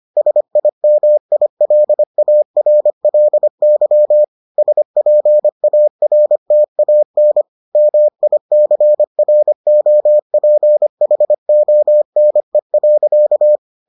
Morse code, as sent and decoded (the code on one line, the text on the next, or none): ... .. -- .. .-.. .- .-. .-.. -.-- / ... .--. .- .-. - .- -. / -- .. -.-. .-. --- .--. .... --- -. .-.-.-
SIMILARLY SPARTAN MICROPHONE.